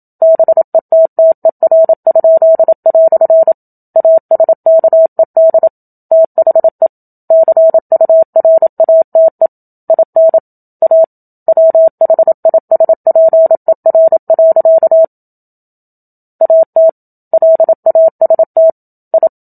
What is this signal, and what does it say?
Morse code, heard as a keyed tone, about 27 wpm, 650 Hz. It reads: BETTER?" ASKED THE CURATE IN A WHISPER. AT LAST I
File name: Book Two / 01-Chapter 1 - 27 WPM